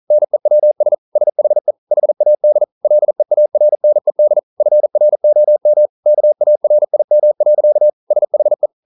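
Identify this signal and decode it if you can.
Morse code, heard as a keyed tone, about 42 wpm, 600 Hz. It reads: NEWS SHE HAD LEARNED FROM KARIM. SHE
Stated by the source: Morse code practice at 41 wpm, from CW med Gnister.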